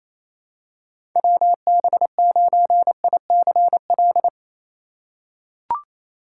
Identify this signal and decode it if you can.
Morse code, heard as a keyed tone, about 28 wpm, 700 Hz. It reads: WB9ICL
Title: USA Call Signs - 6 Characters - 3x Character Spacing - 28wpm